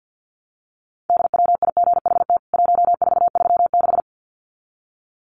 Morse code, heard as a keyed tone, about 48 wpm, 700 Hz. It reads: DWIGHT 1436
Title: CWT Exchange 50wpm